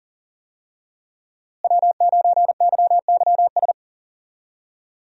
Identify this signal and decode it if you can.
Morse code, heard as a keyed tone, about 37 wpm, 700 Hz. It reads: W9YYS
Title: USA Call Signs - 5 Characters - Repeated 2x - 40wpm